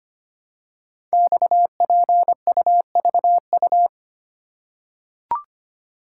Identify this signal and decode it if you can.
Morse code, heard as a keyed tone, about 25 wpm, 700 Hz. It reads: XPUVU
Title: Code Groups - 5 Letters - 25wpm